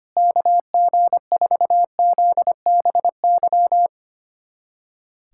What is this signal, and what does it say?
Morse code, heard as a keyed tone, about 25 wpm, 700 Hz. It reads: KG4ZBY
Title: USA Callsign DE Callsign 25wpm